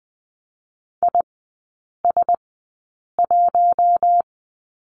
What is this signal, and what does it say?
Morse code, heard as a keyed tone, about 20 wpm, spaced as for 8 wpm, 700 Hz. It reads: IS1
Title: 041 Lesson 8e - Number 1 - Three Characters 20wpm